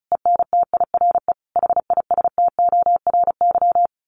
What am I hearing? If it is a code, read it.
Morse code, heard as a keyed tone, about 34 wpm, 700 Hz. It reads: ENTIRE HISTORY
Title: A Sound Of Thunder 35wpm